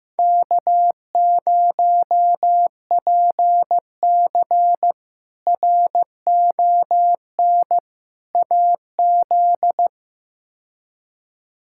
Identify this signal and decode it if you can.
Morse code, heard as a keyed tone, about 15 wpm, 700 Hz. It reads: K0PC RON AZ
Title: K1USN SST QSO Practice - 15wpm